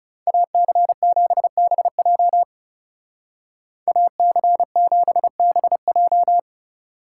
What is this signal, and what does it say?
Morse code, heard as a keyed tone, about 37 wpm, 700 Hz. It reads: AC7BJ AC7BJ
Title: Speed Racing - USA Call Signs 30wpm